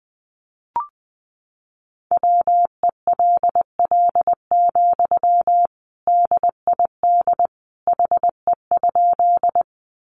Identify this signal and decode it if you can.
Morse code, heard as a keyed tone, about 20 wpm, 700 Hz. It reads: WELL, DID HE?
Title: Sentences from Top 200 Words 20wpm